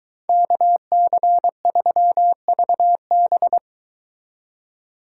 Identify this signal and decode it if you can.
Morse code, heard as a keyed tone, about 22 wpm, 700 Hz. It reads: KC3VB